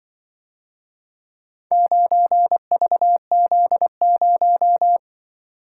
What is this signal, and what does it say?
Morse code, heard as a keyed tone, about 24 wpm, 700 Hz. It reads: 9VZ0
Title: Speed-Racing - Four Letter-Number Combinations - 20wpm